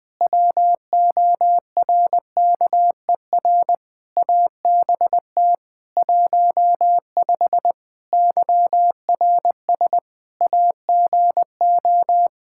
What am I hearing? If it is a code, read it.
Morse code, heard as a keyed tone, about 20 wpm, 700 Hz. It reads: WORKER ABT 15 YRS AGO
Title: Practice QSO - Exchange 3 - More Info - 20wpm